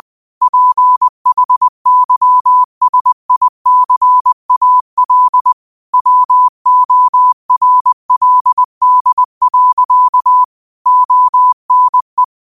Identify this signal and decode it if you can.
Morse code, heard as a keyed tone, about 20 wpm, 1000 Hz. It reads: PHYSICAL WORLD. ONE
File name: qotdfiles / 20wpm